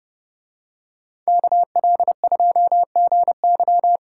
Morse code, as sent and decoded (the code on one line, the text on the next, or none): -.- .-.. ..--- --. -.--
KL2GY